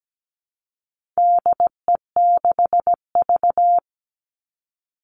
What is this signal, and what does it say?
Morse code, heard as a keyed tone, about 17 wpm, 700 Hz. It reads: DE6V
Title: All Cut Numbers - 4-Digit 17wpm